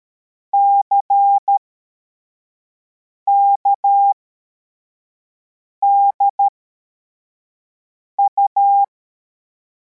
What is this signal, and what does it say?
Morse code, heard as a keyed tone, about 13 wpm, spaced as for 4 wpm, 800 Hz. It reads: CKDU
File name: formation / cw / Text71